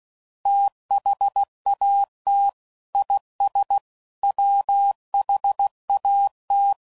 Morse code, heard as a keyed tone, about 16 wpm, 800 Hz. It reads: THAT IS WHAT